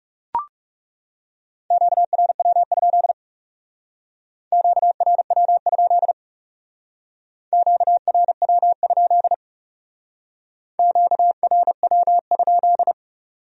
Morse code, heard as a keyed tone, about 38 wpm, 700 Hz. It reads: QRW? QRW? QRW? QRW?